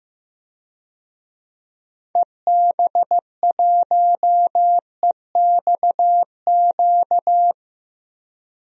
Morse code, heard as a keyed tone, about 15 wpm, 700 Hz. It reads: EB1EXQ
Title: DXCC-100 Call Signs - 6 Characters - 15wpm